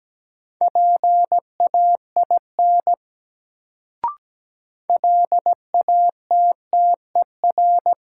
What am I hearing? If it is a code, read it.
Morse code, heard as a keyed tone, about 17 wpm, 700 Hz. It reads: PAIN LATTER
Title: Sentences from Top 1600 Words - New Word Drill Alt - 17wpm